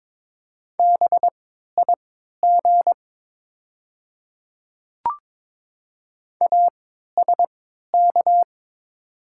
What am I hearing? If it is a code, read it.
Morse code, heard as a keyed tone, about 22 wpm, spaced as for 13 wpm, 690 Hz. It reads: BIG ASK
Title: 3x Character Spacing - Top 200 Words - 22wpm